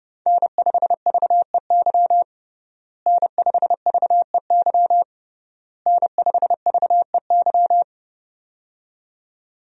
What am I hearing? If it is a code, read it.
Morse code, heard as a keyed tone, about 30 wpm, 700 Hz. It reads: N5VEY N5VEY N5VEY
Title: USA Call Signs - 5 Characters - Repeated 3x - 30wpm